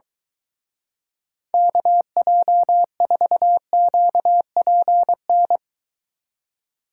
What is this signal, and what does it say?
Morse code, heard as a keyed tone, about 23 wpm, 700 Hz. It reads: KJ4QPN